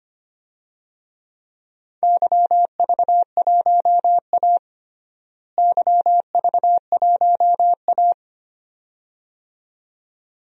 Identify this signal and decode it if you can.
Morse code, heard as a keyed tone, about 25 wpm, 700 Hz. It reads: YV1A YV1A
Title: DXCC-100 Call Signs - 4 Characters - Repeated 2x - 25wpm